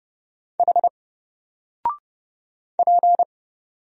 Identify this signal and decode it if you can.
Morse code, heard as a keyed tone, about 30 wpm, 700 Hz. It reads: H P
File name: Single-Letter-Number-30wpm